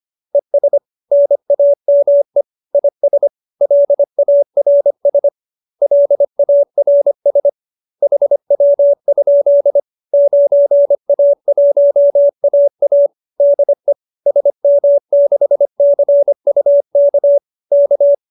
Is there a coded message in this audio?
Yes: Morse code from 0.3 to 18.1 s, heard as a keyed tone, about 25 wpm, 570 Hz. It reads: ES NAME IS LARS LARS HW? 9A1AA DE SM6CUK K